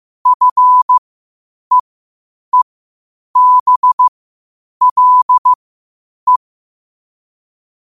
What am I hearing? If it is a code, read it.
Morse code, heard as a keyed tone, about 15 wpm, spaced as for 8 wpm, 1000 Hz. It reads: FEEBLE
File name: qotdfiles / 8wpm